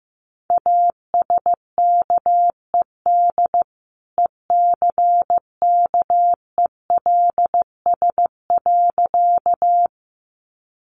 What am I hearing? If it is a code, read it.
Morse code, heard as a keyed tone, about 15 wpm, 700 Hz. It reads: ASKED ECKELS.
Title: A Sound Of Thunder 15wpm